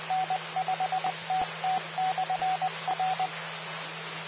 Morse code, heard as a keyed tone, about 20 wpm, 750 Hz, with radio noise. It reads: N5TT/R
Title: Austin, TX Amateur Repeaters